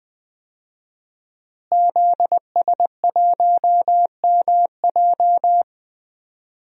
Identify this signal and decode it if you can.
Morse code, heard as a keyed tone, about 20 wpm, 700 Hz. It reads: ZS1MJ
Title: DXCC-100 Call Signs - 5 Characters - Repeated 2x - 20wpm